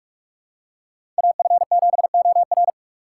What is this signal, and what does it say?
Morse code, heard as a keyed tone, about 46 wpm, 700 Hz. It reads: AF7OR
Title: USA Call Signs - 5 Characters - 2x Character Spacing - 45wpm